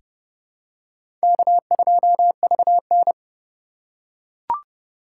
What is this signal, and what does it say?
Morse code, heard as a keyed tone, about 30 wpm, 700 Hz. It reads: K2VN
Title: USA Call Signs - 8x Character Spacing - 30wpm